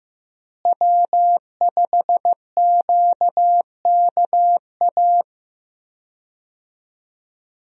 Morse code, heard as a keyed tone, about 15 wpm, 700 Hz. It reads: W5QKA
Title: USA Call Signs - 5 Characters - Repeated 2x - 15wpm